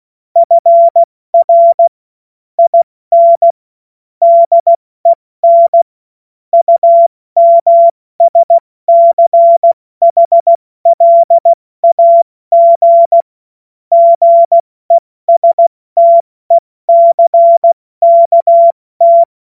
Morse code, heard as a keyed tone, about 16 wpm, 680 Hz. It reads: FR IN DEN UMSCHLAG GESTECKT
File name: Morsekurs / Witze / Witz 06. September 2022 70 F80